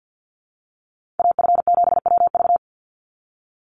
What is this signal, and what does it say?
Morse code, heard as a keyed tone, about 50 wpm, 700 Hz. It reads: AF7WV
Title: USA Call Signs 50wpm